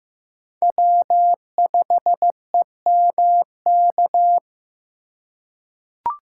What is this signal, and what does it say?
Morse code, heard as a keyed tone, about 15 wpm, 700 Hz. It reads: W5EMK
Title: USA Call Signs - 5 Characters - 4x Character Spacing - 15wpm